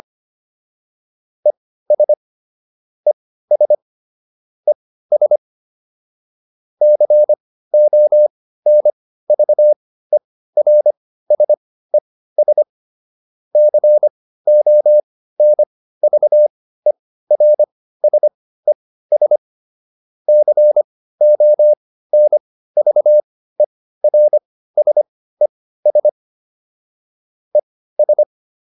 Morse code, heard as a keyed tone, about 25 wpm, spaced as for 15 wpm, 600 Hz. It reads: ES ES ES CONVERSES CONVERSES CONVERSES ES